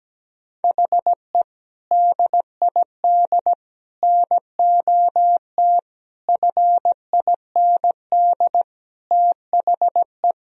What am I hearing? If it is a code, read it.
Morse code, heard as a keyed tone, about 17 wpm, 700 Hz. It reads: HE DID NOT FIND THE
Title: Sentences from Top 1000 Words - Review Part 3 - 17wpm